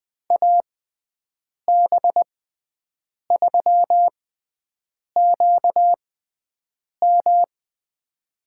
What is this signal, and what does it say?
Morse code, heard as a keyed tone, about 20 wpm, spaced as for 7 wpm, 700 Hz. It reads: AB3QM
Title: USA Call Signs - 6x Character Spacing - 20wpm